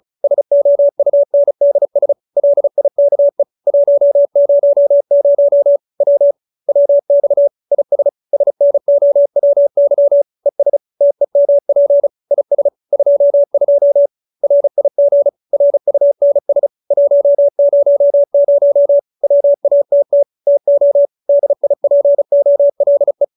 Morse code, heard as a keyed tone, about 34 wpm, 570 Hz. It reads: SOUNDS LIKE 100 W WX IS SNOWY ES TEMP IS 22 RIG RUNS 100 WATT TO DIPOLE